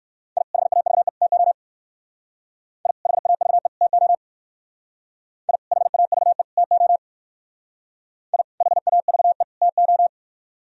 Morse code, heard as a keyed tone, about 57 wpm, 700 Hz. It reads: I HAVE TO I HAVE TO I HAVE TO I HAVE TO